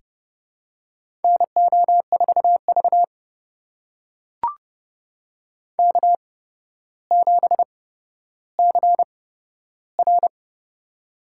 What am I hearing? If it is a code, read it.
Morse code, heard as a keyed tone, about 30 wpm, spaced as for 13 wpm, 700 Hz. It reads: NO4V K7CR
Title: USA Call Signs - 4 Characters - 8x Character Spacing - 30wpm